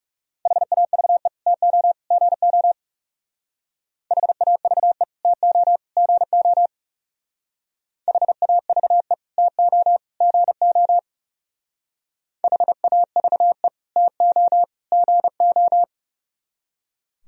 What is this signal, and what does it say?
Morse code, heard as a keyed tone, about 40 wpm, 700 Hz. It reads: HAVE TO GO HAVE TO GO HAVE TO GO HAVE TO GO